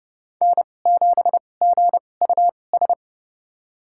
Morse code, heard as a keyed tone, about 30 wpm, spaced as for 22 wpm, 700 Hz. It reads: N7GUS
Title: USA Call Signs - 2x Character Spacing - 30wpm